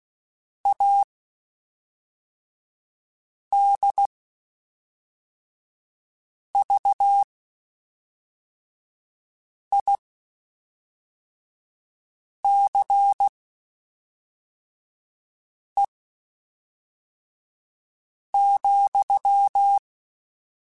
Morse code, heard as a keyed tone, about 16 wpm, spaced as for 3 wpm, 800 Hz. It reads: ADVICE,
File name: Quotes Spaced / Quotes Spaced-mp3s / Advice sp @5